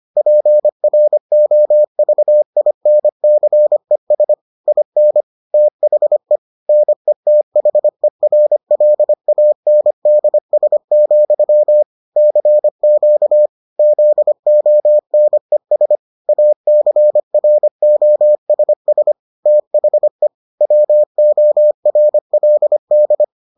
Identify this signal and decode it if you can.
Morse code, heard as a keyed tone, about 25 wpm, 600 Hz. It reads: PROVINCES IN THE NETHERLANDS, CQ ZONES ACROSS THE WORLD